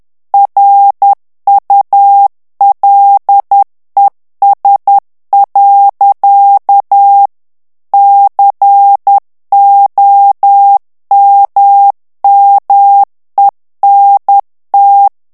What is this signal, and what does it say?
Morse code, heard as a keyed tone, about 10 wpm, 800 Hz. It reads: RULES. COMMENT
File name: ANGLAIS texte clair 3 vitesse 10 mots minute